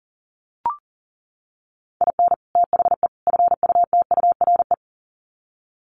Morse code, heard as a keyed tone, about 40 wpm, 700 Hz. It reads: IN THE FUTURE
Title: Sets of 3 Words 40wpm